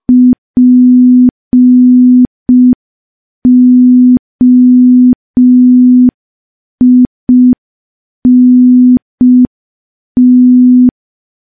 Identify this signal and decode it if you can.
Morse code, heard as a keyed tone, about 5 wpm, 250 Hz. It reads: POINT
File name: morse / morse-training-05WPM